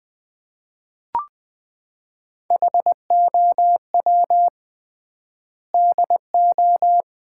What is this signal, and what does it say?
Morse code, heard as a keyed tone, about 21 wpm, 700 Hz. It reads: HOW DO
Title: 3x Word-Spacing - Sets of 5 Words - 20wpm